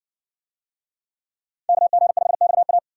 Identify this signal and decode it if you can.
Morse code, heard as a keyed tone, about 60 wpm, 700 Hz. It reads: KG5XA